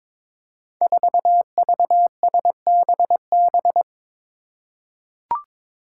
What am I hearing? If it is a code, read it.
Morse code, heard as a keyed tone, about 22 wpm, 700 Hz. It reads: 4VSBB
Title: Five Challenging Letter-Number Combinations 22wpm